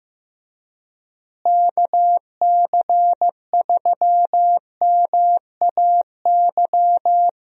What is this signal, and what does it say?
Morse code, heard as a keyed tone, about 15 wpm, 700 Hz. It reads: KC3MAY
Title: USA Call Signs - 6 Characters - Repeated 3x - 15wpm